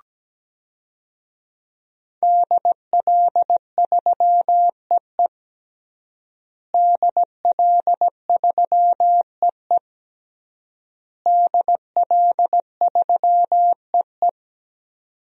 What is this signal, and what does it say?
Morse code, heard as a keyed tone, about 17 wpm, 700 Hz. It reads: DL3EE DL3EE DL3EE
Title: DXCC-100 Call Signs - 5 Characters - Repeated 3x - 17wpm